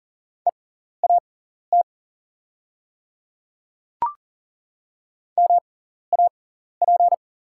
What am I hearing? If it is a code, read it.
Morse code, heard as a keyed tone, about 40 wpm, spaced as for 14 wpm, 690 Hz. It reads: EAT MAP
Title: 6x Character Spacing - Top 500 Words - 40wpm